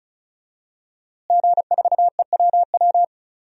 Morse code, heard as a keyed tone, about 34 wpm, 700 Hz. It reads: G4EWW
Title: DXCC-100 Call Signs - 5 Characters - Repeated 2x - 35wpm